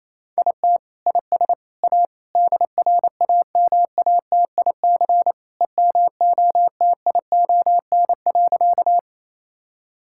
Morse code, heard as a keyed tone, about 28 wpm, 700 Hz. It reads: IT IS A DRAMATIC EMOTION.
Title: Sentences from Top 1600 Words 28wpm